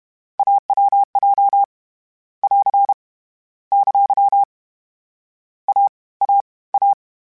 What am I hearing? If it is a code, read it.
Morse code, heard as a keyed tone, about 31 wpm, 800 Hz. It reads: AWJ <AR> ! A A A
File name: Difficult Combos / Difficult Combos-mp3s / AWJ1 @30